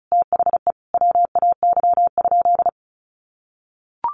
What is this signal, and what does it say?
Morse code, heard as a keyed tone, about 34 wpm, 700 Hz. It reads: THE WAY?
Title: Sentences from Top 1100 Words 35wpm